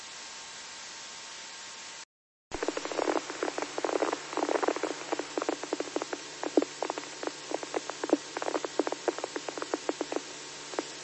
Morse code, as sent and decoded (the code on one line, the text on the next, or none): none